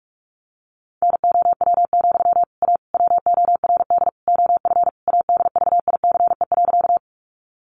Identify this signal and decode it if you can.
Morse code, heard as a keyed tone, about 46 wpm, 700 Hz. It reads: NOW, A WORD OF ADVICE.
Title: Sentences from Top 1100 Words 45wpm